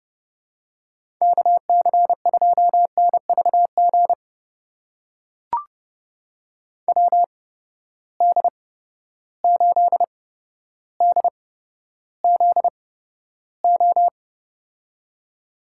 Morse code, heard as a keyed tone, about 31 wpm, spaced as for 13 wpm, 700 Hz. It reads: KC2NVG WD8DZO
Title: USA Call Signs - 6 Characters - 8x Character Spacing - 30wpm